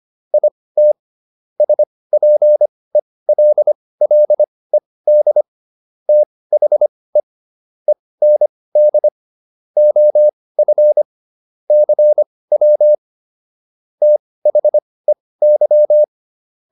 Morse code, heard as a keyed tone, about 25 wpm, spaced as for 18 wpm, 600 Hz. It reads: IT SPELLED THE END OF CW THEY